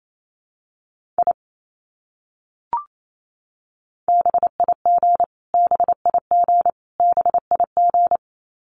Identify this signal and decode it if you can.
Morse code, heard as a keyed tone, about 28 wpm, 700 Hz. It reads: I BIG BIG BIG
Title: Top 300 Words Repeated 28wpm